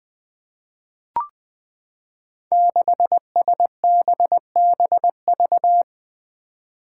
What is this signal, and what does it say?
Morse code, heard as a keyed tone, about 20 wpm, 700 Hz. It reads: 6SBBV
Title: Five Challenging Letter-Number Combinations 20wpm